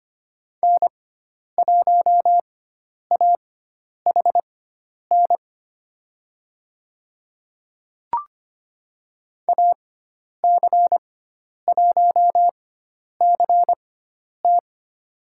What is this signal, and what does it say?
Morse code, heard as a keyed tone, about 25 wpm, spaced as for 10 wpm, 700 Hz. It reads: N1AHN AC1CT
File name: Lesson 16g - Letter c - Call Signs 25wpm